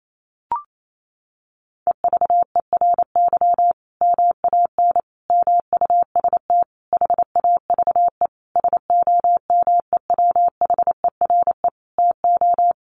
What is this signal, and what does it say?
Morse code, heard as a keyed tone, about 28 wpm, 700 Hz. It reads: EVERY MAN MUST HAVE SOMEWHERE TO
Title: Sentences from Top 1400 Words 28wpm